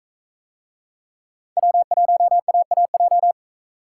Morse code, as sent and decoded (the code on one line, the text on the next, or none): .-- .---- .- .- .---
W1AAJ